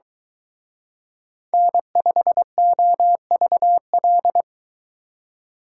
Morse code, heard as a keyed tone, about 23 wpm, 700 Hz. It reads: N5OVL